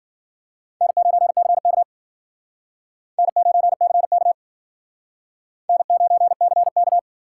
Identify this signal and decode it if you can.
Morse code, heard as a keyed tone, about 51 wpm, 700 Hz. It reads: N9CK N9CK N9CK